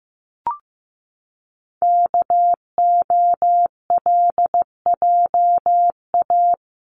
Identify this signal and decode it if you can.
Morse code, heard as a keyed tone, about 15 wpm, 700 Hz. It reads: KOLJA
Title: CWT Exchange 15wpm